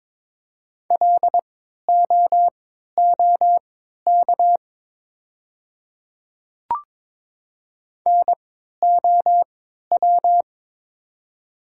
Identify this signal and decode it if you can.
Morse code, heard as a keyed tone, about 22 wpm, spaced as for 13 wpm, 700 Hz. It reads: LOOK NOW
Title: 3x Character Spacing - Top 100 Words - 22wpm